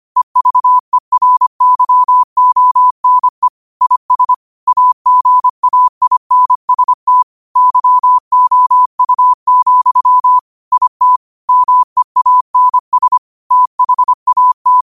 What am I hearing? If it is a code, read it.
Morse code, heard as a keyed tone, about 25 wpm, 1000 Hz. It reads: EVERYONE IS AGAINST YOU, IT MEANS THAT
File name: qotdfiles / 25wpm